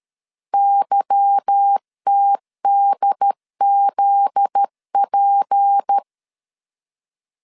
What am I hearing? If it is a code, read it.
Morse code, heard as a keyed tone, about 13 wpm, 800 Hz. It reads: YTDZP
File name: LEZIONI / 8 YJ45 / 8 YJ4 12wpm 2151